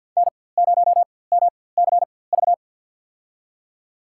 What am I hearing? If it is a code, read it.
Morse code, heard as a keyed tone, about 52 wpm, spaced as for 24 wpm, 700 Hz. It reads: N0MCV
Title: USA Call Signs - 5 Characters - 4x Character Spacing - 50wpm